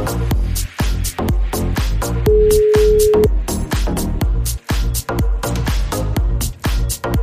Ringback tone (Europe, 425 Hz).